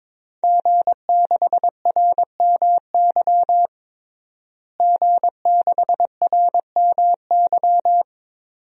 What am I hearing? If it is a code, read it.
Morse code, heard as a keyed tone, about 22 wpm, 700 Hz. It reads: G6RMY G6RMY
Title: DXCC-100 Call Signs - 5 Characters - Repeated 2x - 22wpm